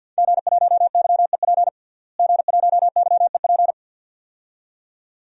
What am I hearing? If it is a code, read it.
Morse code, heard as a keyed tone, about 51 wpm, 700 Hz. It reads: G1YEP G1YEP
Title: DXCC-100 Call Signs - 5 Characters - Repeated 2x - 50wpm